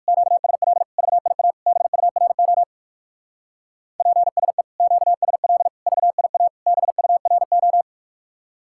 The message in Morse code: QSL VIA BURO PSE QSL VIA BURO

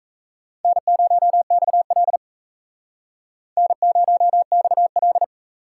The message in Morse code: N0XL N0XL

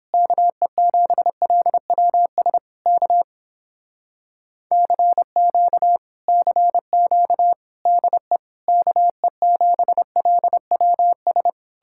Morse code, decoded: KE7LWS K CQ CQ DE KE7LWS

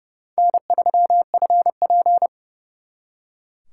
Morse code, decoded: N3FP

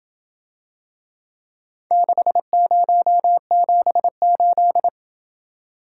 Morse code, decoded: 6078

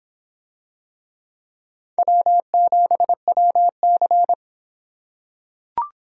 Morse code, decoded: W7WC

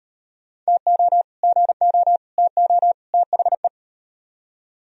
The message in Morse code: TO GO TO THE

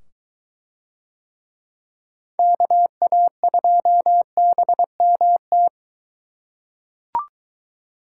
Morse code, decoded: KA2BMT

